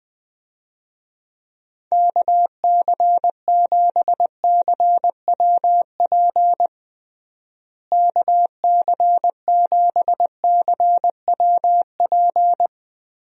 Morse code, heard as a keyed tone, about 20 wpm, 700 Hz. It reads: KC7CWP KC7CWP
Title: USA Call Signs - 6 Characters - Repeated 3x - 20wpm